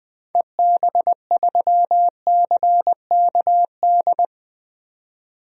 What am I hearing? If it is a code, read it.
Morse code, heard as a keyed tone, about 20 wpm, 700 Hz. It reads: EB3CKD
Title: DXCC-100 Call Signs - 6 Characters - 20wpm